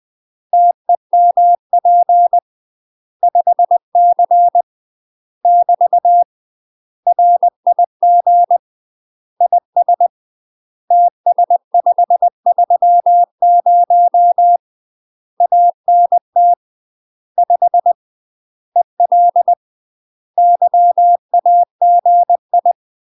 Morse code, heard as a keyed tone, about 20 wpm, 700 Hz. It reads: TEMP 5C <BT> RIG IS TS530 ANT 5 EL YAGI
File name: UK MASTER QSOs TRAINING / 20WPM / UK MASTER QSOs 0051